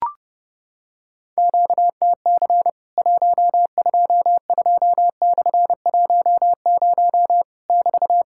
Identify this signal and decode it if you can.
Morse code, heard as a keyed tone, about 30 wpm, 700 Hz. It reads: QTC 122/10 <BT>
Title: Worked All Europe DX Contest - QTC Exchange - 30wpm